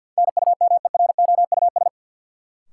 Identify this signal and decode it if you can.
Morse code, heard as a keyed tone, about 47 wpm, 700 Hz. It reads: NUMEROUS
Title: Sentences from Top 1400 Words - New Word Drill Alt - 50wpm